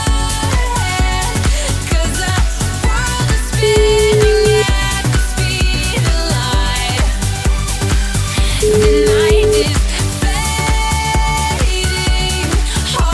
Ringback tone (Europe, 425 Hz), with noise.